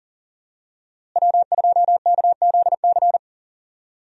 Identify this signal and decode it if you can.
Morse code, heard as a keyed tone, about 40 wpm, 700 Hz. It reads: W2KZC
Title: USA Call Signs - 5 Characters - Repeated 3x - 40wpm